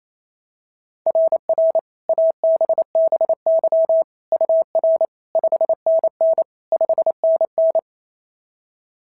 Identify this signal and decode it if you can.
Morse code, heard as a keyed tone, about 28 wpm, 690 Hz. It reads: RR ABBY UR 5NN 5NN
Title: SOTA QSO Practice 28wpm